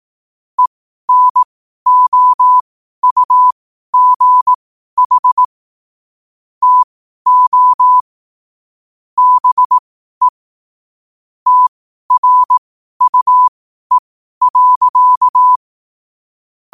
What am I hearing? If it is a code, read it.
Morse code, heard as a keyed tone, about 18 wpm, spaced as for 12 wpm, 1000 Hz. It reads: ENOUGH TO BE TRUE.
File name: qotdfiles / 12wpm